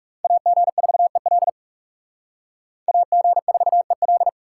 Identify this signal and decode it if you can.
Morse code, heard as a keyed tone, about 37 wpm, 700 Hz. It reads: AG4EL AG4EL
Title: Speed Racing - USA 2x2 Call Signs 30wpm